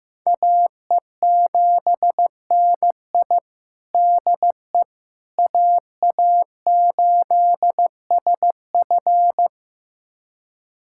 Morse code, heard as a keyed tone, about 15 wpm, 700 Hz. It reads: AE7NI DE AA8SF
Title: USA Callsign DE Callsign - 5 Characters - 15wpm